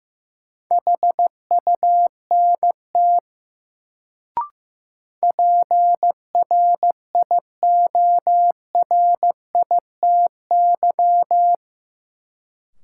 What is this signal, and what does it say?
Morse code, heard as a keyed tone, about 15 wpm, 700 Hz. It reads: HUNT PRIORITY
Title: Sentences from Top 1600 Words - New Word Drill - 15wpm